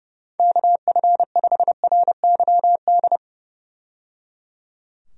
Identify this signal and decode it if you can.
Morse code, heard as a keyed tone, about 30 wpm, 700 Hz. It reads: KF5RYD
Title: USA Call Signs - 6 Characters - 30wpm